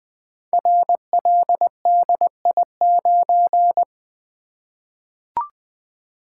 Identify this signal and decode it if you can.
Morse code, heard as a keyed tone, about 20 wpm, 700 Hz. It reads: RLDI9
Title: Code Groups - 5 Letter:Numbers - 20wpm